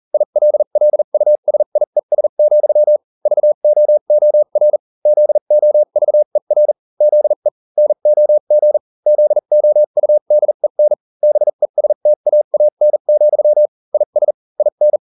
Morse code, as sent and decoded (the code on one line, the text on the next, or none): .. .-.. .-.. ..- ... .. ... --..-- / ...- --- --- .-. / --.. --- ...- . .-. / --.. / -. --- --. / --.. --- ..- -.. -. / -... ... - .- .- -. --..-- / .. ... / .. -.
ILLUSIES, VOOR ZOVER ZE NOG ZOUDEN BESTAAN, IS IN